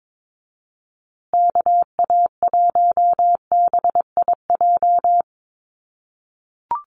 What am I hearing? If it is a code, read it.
Morse code, heard as a keyed tone, about 22 wpm, 700 Hz. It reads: KA1BIJ
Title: USA Call Signs Repeated 2x 22wpm